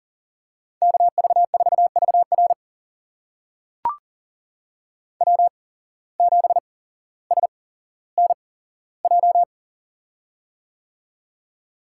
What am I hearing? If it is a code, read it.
Morse code, heard as a keyed tone, about 40 wpm, spaced as for 17 wpm, 700 Hz. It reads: KV4VR W7SNJ
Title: USA Call Signs - 5 Characters - 8x Character Spacing - 40wpm